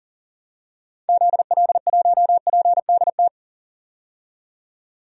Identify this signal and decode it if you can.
Morse code, heard as a keyed tone, about 40 wpm, 700 Hz. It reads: ZL1PDT
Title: DXCC-100 Call Signs - 6 Characters - 40wpm